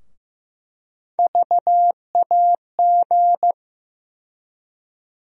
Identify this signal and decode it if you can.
Morse code, heard as a keyed tone, about 15 wpm, 700 Hz. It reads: VAG
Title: USA Call Sign Postfixes - 3 Letters 15wpm